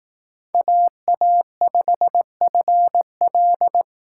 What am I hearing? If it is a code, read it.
Morse code, heard as a keyed tone, about 18 wpm, 700 Hz. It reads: AA5FL
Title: Speed Racing - USA 2x2 Call Signs 15wpm